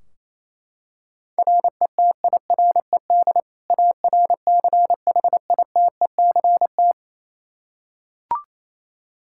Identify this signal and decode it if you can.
Morse code, heard as a keyed tone, about 28 wpm, 700 Hz. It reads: RETIRED ARCHITECT
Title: Practice QSO - Exchange 3 - More Info - 28wpm